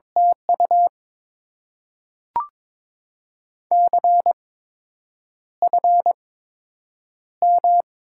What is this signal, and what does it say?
Morse code, heard as a keyed tone, about 22 wpm, spaced as for 8 wpm, 700 Hz. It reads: TU CFM
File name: Top 90 QSO Elements - 8x Character Spacing - 22wpm